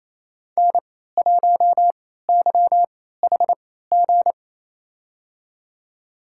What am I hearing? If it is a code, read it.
Morse code, heard as a keyed tone, about 28 wpm, spaced as for 16 wpm, 700 Hz. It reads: N1YHG
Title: USA Call Signs - 3x Character Spacing - 28wpm